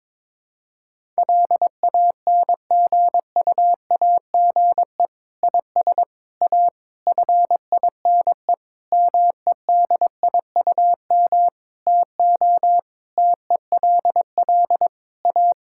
Morse code, heard as a keyed tone, about 21 wpm, 700 Hz. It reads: LANGUAGE IS A FINE MEDIUM TO TELL A
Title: Sentences from Top 1000 Words - Review Part 6 - 22wpm